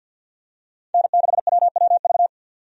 Morse code, heard as a keyed tone, about 50 wpm, 700 Hz. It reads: N6WWV
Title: USA Call Signs - 5 Characters - 50wpm